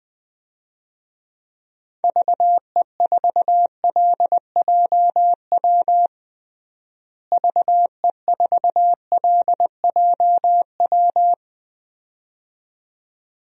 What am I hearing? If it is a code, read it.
Morse code, heard as a keyed tone, about 21 wpm, 700 Hz. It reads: VE4LJW VE4LJW
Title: DXCC-100 Call Signs - 6 Characters - Repeated 2x - 20wpm